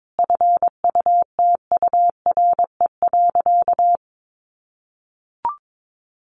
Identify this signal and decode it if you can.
Morse code, heard as a keyed tone, about 22 wpm, 700 Hz. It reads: FUTURE.
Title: Sentences from Top 1200 Words 22wpm